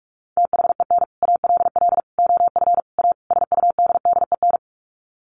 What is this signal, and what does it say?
Morse code, heard as a keyed tone, about 44 wpm, 700 Hz. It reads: THEN ALL OF A SUDDEN
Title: Sets of 5 Words Encore 45wpm